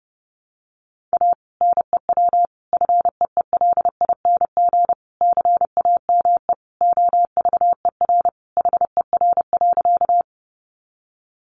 Morse code, heard as a keyed tone, about 30 wpm, 700 Hz. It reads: A NEW FEELING CAME OVER HER.
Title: Sentences from Top 1200 Words 30wpm